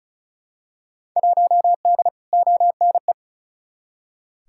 Morse code, heard as a keyed tone, about 36 wpm, 700 Hz. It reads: 1D ONE
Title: Field Day Exchanges - 35wpm